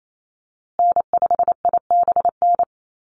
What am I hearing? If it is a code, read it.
Morse code, heard as a keyed tone, about 28 wpm, 700 Hz. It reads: N5IBN
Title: USA Call Signs Repeated 2x 28wpm